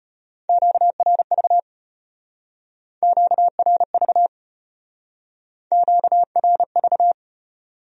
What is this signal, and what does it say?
Morse code, heard as a keyed tone, about 34 wpm, 700 Hz. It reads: QRV QRV QRV